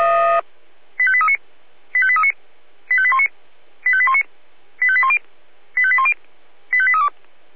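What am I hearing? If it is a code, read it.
Five-tone selective call (ZVEI), heard as quick runs of five tones, tones of about 70 ms. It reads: 85839 85839 85819 85819 85810 85810 85832
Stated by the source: Five-tone ZVEI selective call 85278 received by SDR.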